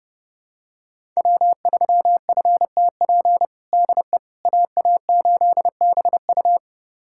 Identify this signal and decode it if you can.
Morse code, heard as a keyed tone, about 32 wpm, 700 Hz. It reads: W3FTP DE AA8BU
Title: USA Callsign DE Callsign - 5 Characters - 30wpm